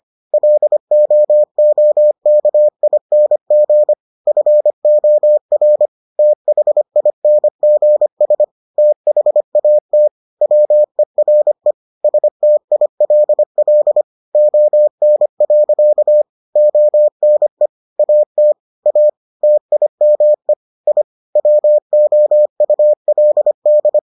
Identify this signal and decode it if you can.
Morse code, heard as a keyed tone, about 25 wpm, 600 Hz. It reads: LOOKING FOR THINGS THAT WERE STILL ON. ONE AT A TIME I WOULD